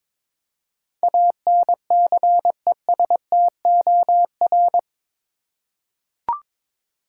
Morse code, heard as a keyed tone, about 21 wpm, 700 Hz. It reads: ANCESTOR
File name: practice / 22wpm / Eight Letter Words - 22wpm